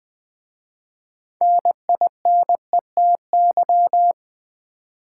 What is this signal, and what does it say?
Morse code, heard as a keyed tone, about 20 wpm, 700 Hz. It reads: NINETY
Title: 6 Letter Words - First 3 Letters Given - 20wpm